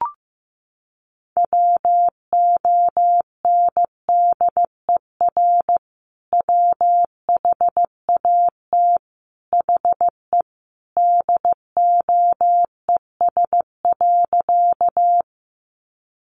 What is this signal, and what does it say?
Morse code, heard as a keyed tone, about 15 wpm, 700 Hz. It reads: WONDER WHAT HE DOES.
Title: Sentences from Top 500 Words 15wpm